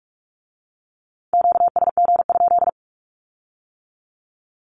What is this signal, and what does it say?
Morse code, heard as a keyed tone, about 46 wpm, 700 Hz. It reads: QSG?
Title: Q-Codes with Spoken Meaning - 45wpm